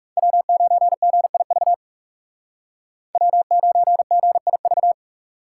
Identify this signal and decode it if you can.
Morse code, heard as a keyed tone, about 38 wpm, 700 Hz. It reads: W9GIV W9GIV